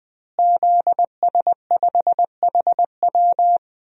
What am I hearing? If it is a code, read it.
Morse code, heard as a keyed tone, about 20 wpm, 700 Hz. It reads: ZS5HW
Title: DXCC-100 Call Signs - 5 Characters - Repeated 3x - 20wpm